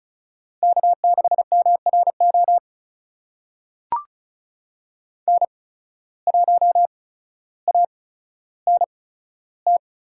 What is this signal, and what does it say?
Morse code, heard as a keyed tone, about 36 wpm, spaced as for 15 wpm, 700 Hz. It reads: K6MRO N1ANT
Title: USA Call Signs - 5 Characters - 8x Character Spacing - 35wpm